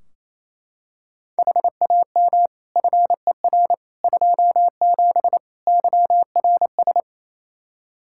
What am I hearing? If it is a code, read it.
Morse code, heard as a keyed tone, about 28 wpm, 700 Hz. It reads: HAM FER 27 YRS
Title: Practice QSO - Exchange 3 - More Info - 28wpm